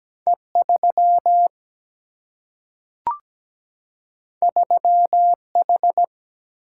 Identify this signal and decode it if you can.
Morse code, heard as a keyed tone, about 17 wpm, 700 Hz. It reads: E 3 3 H
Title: Single-Letter Single-Number 17wpm